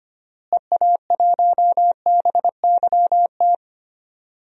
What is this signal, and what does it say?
Morse code, heard as a keyed tone, about 25 wpm, 700 Hz. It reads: EA1BYT
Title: DXCC-100 Call Signs - 6 Characters - Repeated 3x - 25wpm